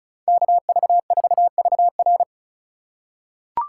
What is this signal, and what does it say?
Morse code, heard as a keyed tone, about 34 wpm, 700 Hz. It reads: KV4VR
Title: USA Call Signs - 2x Character Spacing - 35wpm